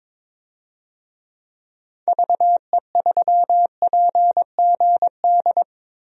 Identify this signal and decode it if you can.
Morse code, heard as a keyed tone, about 21 wpm, 700 Hz. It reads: VE3PGD
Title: DXCC-100 Call Signs - 6 Characters - 22wpm